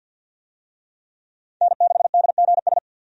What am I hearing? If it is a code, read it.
Morse code, heard as a keyed tone, about 52 wpm, 700 Hz. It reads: N6DGS
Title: USA Call Signs - 5 Characters - 50wpm